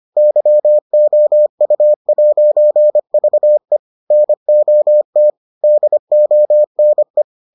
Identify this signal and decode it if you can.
Morse code, heard as a keyed tone, about 25 wpm, 600 Hz. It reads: YOU'VE NOT DONE